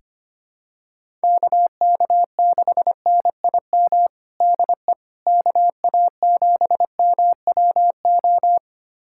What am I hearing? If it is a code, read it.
Morse code, heard as a keyed tone, about 25 wpm, 700 Hz. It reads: KK6NIM DE KA7MWO
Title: USA Callsign DE Callsign - 6 Characters - 25wpm